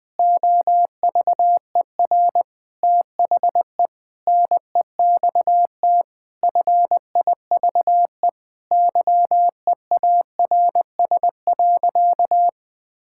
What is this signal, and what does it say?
Morse code, heard as a keyed tone, about 20 wpm, 700 Hz. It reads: OVER THE NEXT FIVE YEARS.
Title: Sentences with 5-Word Phrases - 20wpm